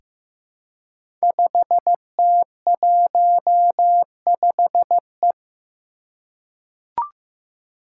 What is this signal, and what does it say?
Morse code, heard as a keyed tone, about 15 wpm, 700 Hz. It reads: 5T15E
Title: Five Challenging Letter-Number Combinations 15wpm